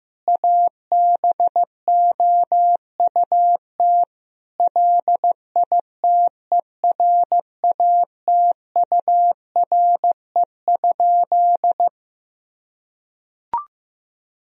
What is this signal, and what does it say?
Morse code, heard as a keyed tone, about 15 wpm, 700 Hz. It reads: ABOUT LITERATURE?
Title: Sentences from Top 1000 Words - Review Part 3 - 15wpm